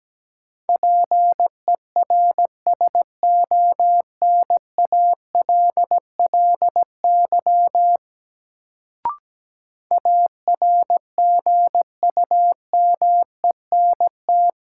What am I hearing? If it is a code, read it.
Morse code, heard as a keyed tone, about 17 wpm, 700 Hz. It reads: PERSONALLY ARGUMENT
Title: Sentences from Top 1400 Words - New Word Drill Alt - 17wpm